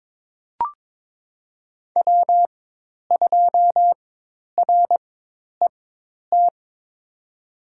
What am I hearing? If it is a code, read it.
Morse code, heard as a keyed tone, about 22 wpm, spaced as for 10 wpm, 700 Hz. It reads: W2RET
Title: USA Call Signs - 5 Characters - 4x Character Spacing - 22wpm